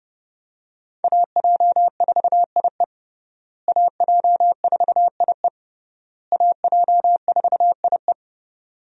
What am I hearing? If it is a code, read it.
Morse code, heard as a keyed tone, about 30 wpm, 700 Hz. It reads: AJ4IE AJ4IE AJ4IE
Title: USA Call Signs - 5 Characters - Repeated 3x - 30wpm